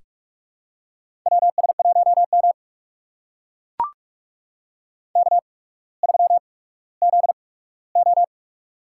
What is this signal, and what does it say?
Morse code, heard as a keyed tone, about 44 wpm, spaced as for 19 wpm, 700 Hz. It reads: WS1M K3ZO
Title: USA Call Signs - 4 Characters - 8x Character Spacing - 45wpm